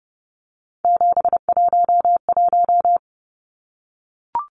Morse code, heard as a keyed tone, about 30 wpm, 700 Hz. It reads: 711
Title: Numbers - 3-Digit 30wpm